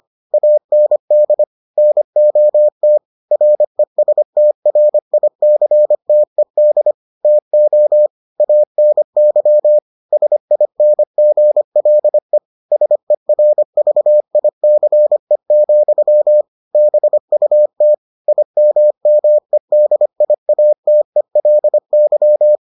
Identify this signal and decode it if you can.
Morse code, heard as a keyed tone, about 25 wpm, 600 Hz. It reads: AND NOT RESTRICTED TO ANY SINGLE SERVICE, BUT IMMEDIATELY